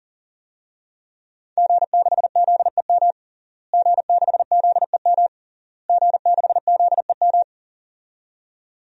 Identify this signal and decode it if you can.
Morse code, heard as a keyed tone, about 37 wpm, 700 Hz. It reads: G6ZEM G6ZEM G6ZEM
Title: DXCC-100 Call Signs - 5 Characters - Repeated 3x - 40wpm